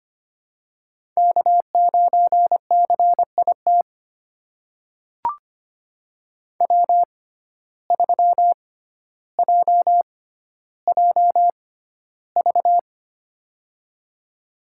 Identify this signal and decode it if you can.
Morse code, heard as a keyed tone, about 25 wpm, spaced as for 13 wpm, 700 Hz. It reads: K9CIT W3JJV
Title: USA Call Signs - 5 Characters - 6x Character Spacing - 25wpm